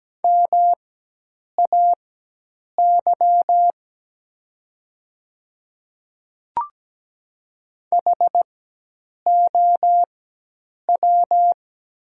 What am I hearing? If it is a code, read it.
Morse code, heard as a keyed tone, about 17 wpm, spaced as for 8 wpm, 700 Hz. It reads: MAY HOW
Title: 4x Character Spacing - Top 100 Words - 17wpm